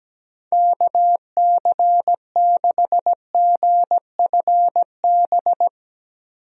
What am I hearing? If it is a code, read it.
Morse code, heard as a keyed tone, about 17 wpm, 700 Hz. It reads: KC6GFB
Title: USA Call Signs - 6 Characters - 17wpm